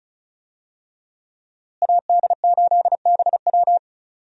Morse code, heard as a keyed tone, about 34 wpm, 700 Hz. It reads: AD8BW
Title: USA Call Signs - 5 Characters - Repeated 2x - 35wpm